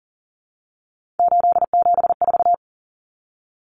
Morse code, heard as a keyed tone, about 40 wpm, 700 Hz. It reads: 874
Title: Numbers - 3-Digit 40wpm